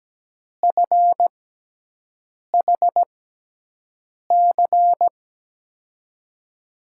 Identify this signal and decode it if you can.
Morse code, heard as a keyed tone, about 17 wpm, spaced as for 6 wpm, 700 Hz. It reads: FHC
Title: USA Call Signs - 5 Characters - 6x Character Spacing - 17wpm